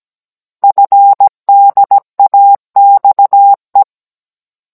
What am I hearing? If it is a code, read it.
Morse code, heard as a keyed tone, about 17 wpm, 800 Hz. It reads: FDAXE
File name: Cours-NR19-16mm